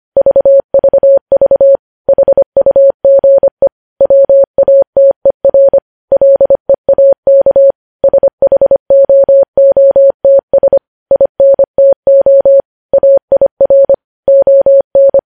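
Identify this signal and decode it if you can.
Morse code, heard as a keyed tone, about 25 wpm, 550 Hz. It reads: VVV HUGE WATER LEAK SHOOTS INTO AIR ON
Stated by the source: Morse code news headlines sent at 25 wpm.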